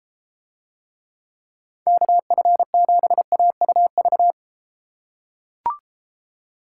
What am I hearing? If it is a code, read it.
Morse code, heard as a keyed tone, about 33 wpm, 700 Hz. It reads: KF7AUV